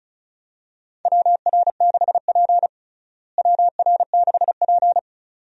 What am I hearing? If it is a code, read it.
Morse code, heard as a keyed tone, about 35 wpm, 700 Hz. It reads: WR6P WR6P
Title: USA Call Signs - 4 Characters - Repeated 2x - 35wpm